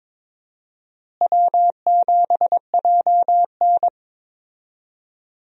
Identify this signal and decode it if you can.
Morse code, heard as a keyed tone, about 22 wpm, 700 Hz. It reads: W7JN
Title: USA Call Signs - 4 Characters - Repeated 3x - 22wpm